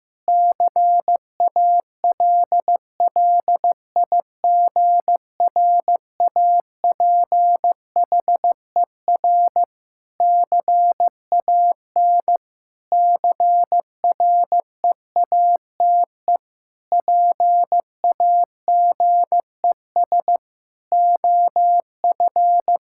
Morse code, heard as a keyed tone, about 15 wpm, 700 Hz. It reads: CALLIGRAPHER CAN CREATE PAGES OF
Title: Pangram Challenge v2 - 15wpm